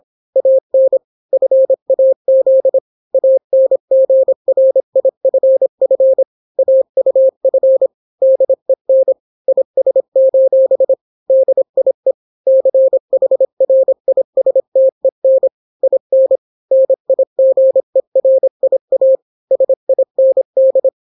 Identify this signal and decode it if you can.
Morse code, heard as a keyed tone, about 26 wpm, 530 Hz. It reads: AN FAZ ANGRIFF AUF DEN IS: DIE CHRISTEN IN NIGERIA SIND